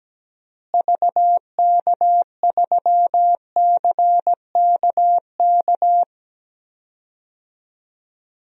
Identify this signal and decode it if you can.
Morse code, heard as a keyed tone, about 17 wpm, 700 Hz. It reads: VK3CKK
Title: DXCC-100 Call Signs - 6 Characters - Repeated 3x - 17wpm